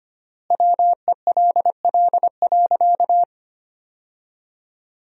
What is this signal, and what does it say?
Morse code, heard as a keyed tone, about 25 wpm, 700 Hz. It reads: WELL.
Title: 3x Word-Spacing - Sentences from Top 300 Words - 25wpm